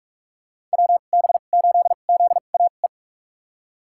Morse code, heard as a keyed tone, about 44 wpm, spaced as for 32 wpm, 700 Hz. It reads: WB8ZAE
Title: USA Call Signs - 6 Characters - 2x Character Spacing - 45wpm